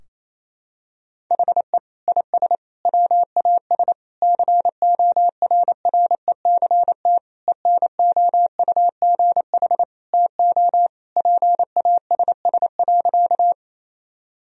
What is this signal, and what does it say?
Morse code, heard as a keyed tone, about 28 wpm, 700 Hz. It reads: HE IS WAS CORRECT ENOUGH TO PASS.
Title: Sentences from Top 1000 Words - Review Part 2 - 28wpm